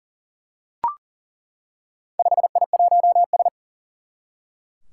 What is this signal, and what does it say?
Morse code, heard as a keyed tone, about 40 wpm, 700 Hz. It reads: 5I1S
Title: Four Challenging Letter-Number Combinations 40wpm